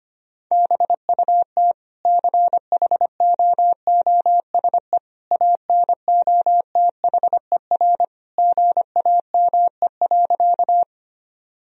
Morse code, heard as a keyed tone, about 25 wpm, 700 Hz. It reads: BUT CHOOSE ANOTHER GAME.
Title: Sentences from Top 1000 Words - Review Part 4 - 25wpm